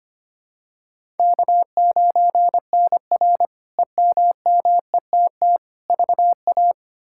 Morse code, heard as a keyed tone, about 25 wpm, 700 Hz. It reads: K9NR EMMETT VA
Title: K1USN SST QSO Practice - 25wpm